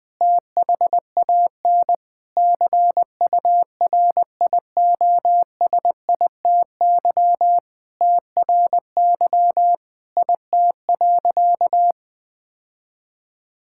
Morse code, heard as a keyed tone, about 20 wpm, 700 Hz. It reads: THAN CURIOSITY TRY IT.
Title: Sentences with 6-Word Phrases - 20wpm